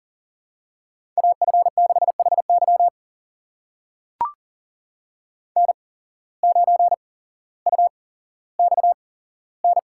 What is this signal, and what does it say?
Morse code, heard as a keyed tone, about 40 wpm, spaced as for 17 wpm, 700 Hz. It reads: AF6HY N9UXN
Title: USA Call Signs - 5 Characters - 8x Character Spacing - 40wpm